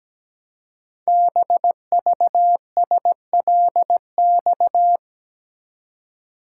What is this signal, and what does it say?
Morse code, heard as a keyed tone, about 17 wpm, 700 Hz. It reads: BVSLX
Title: Code Groups - 5 Letter:Numbers - 17wpm